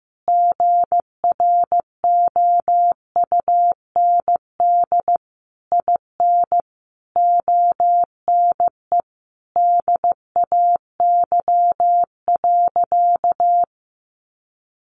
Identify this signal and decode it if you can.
Morse code, heard as a keyed tone, about 15 wpm, 700 Hz. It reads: GROUND IN ONE DAY.
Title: Sentences from Top 500 Words 15wpm